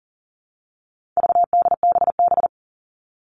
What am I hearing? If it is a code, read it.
Morse code, heard as a keyed tone, about 40 wpm, 700 Hz. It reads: VDBB
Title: All Cut Numbers - 4-Digit 40wpm